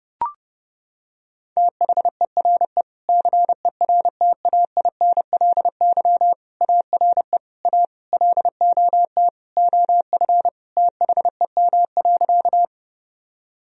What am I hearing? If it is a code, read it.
Morse code, heard as a keyed tone, about 30 wpm, 700 Hz. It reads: THERE CERTAINLY ARE A LOT OF THEM.
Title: Sentences from Top 1000 Words - Review Part 3 - 30wpm